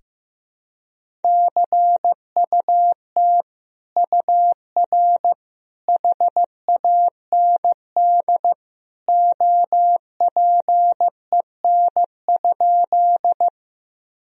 Morse code, heard as a keyed tone, about 15 wpm, 700 Hz. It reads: CUT UR HAND OPEN?
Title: Sentences from Top 300 Words and Abbreviations 15wpm